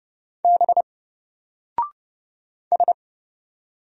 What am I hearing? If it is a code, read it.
Morse code, heard as a keyed tone, about 30 wpm, 700 Hz. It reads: B S
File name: Single-Letter-Number-30wpm